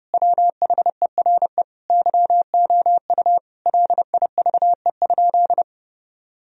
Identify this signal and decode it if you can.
Morse code, heard as a keyed tone, about 30 wpm, 700 Hz. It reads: WHERE YOU LIVE?
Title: Sentences from Top 1000 Words - Review Part 4 - 30wpm